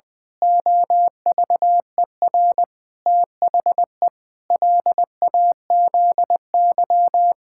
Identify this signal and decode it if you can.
Morse code, heard as a keyed tone, about 20 wpm, 700 Hz. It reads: OVER THE LAZY
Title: Pangram Challenge v2 - 20wpm